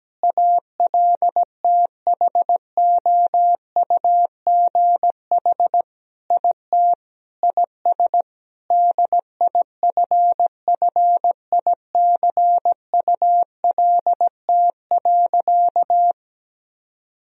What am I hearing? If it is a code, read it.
Morse code, heard as a keyed tone, about 17 wpm, 700 Hz. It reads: ALTHOUGH IT IS DIFFICULT.
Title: Sentences from Top 1000 Words - Review Part 3 - 17wpm